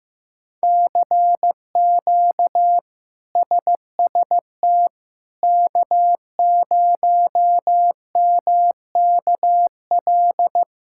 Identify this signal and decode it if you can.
Morse code, heard as a keyed tone, about 16 wpm, 700 Hz. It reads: CQ SST K0MKL
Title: K1USN SST QSO Practice - 15wpm